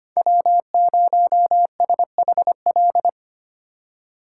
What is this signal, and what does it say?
Morse code, heard as a keyed tone, about 25 wpm, 700 Hz. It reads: W0SHL
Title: K1USN SST QSO Practice - 25wpm